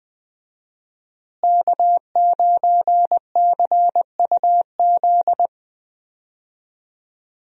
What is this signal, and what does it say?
Morse code, heard as a keyed tone, about 21 wpm, 700 Hz. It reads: K9CUZ
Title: USA Call Signs - 5 Characters - Repeated 2x - 20wpm